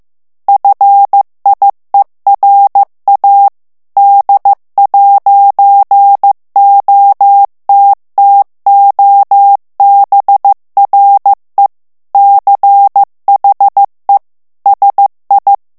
Morse code, heard as a keyed tone, about 15 wpm, 800 Hz. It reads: FIERA D'OTTOBRE CHE SI
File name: ITALIEN texte clair 1 vitesse 14 mots minute